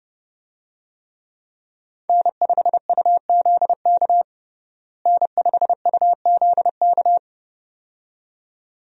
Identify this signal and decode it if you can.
Morse code, heard as a keyed tone, about 30 wpm, 700 Hz. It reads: N5UZK N5UZK
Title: USA Call Signs - 5 Characters - Repeated 2x - 30wpm